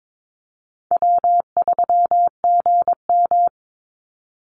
Morse code, heard as a keyed tone, about 22 wpm, 700 Hz. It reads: W3GM
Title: USA Call Signs 22wpm